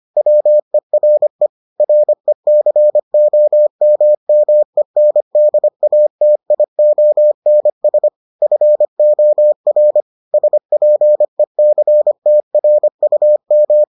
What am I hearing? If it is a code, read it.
Morse code, heard as a keyed tone, about 25 wpm, 600 Hz. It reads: WERE RECOMMENDATIONS FOR SPECTRUM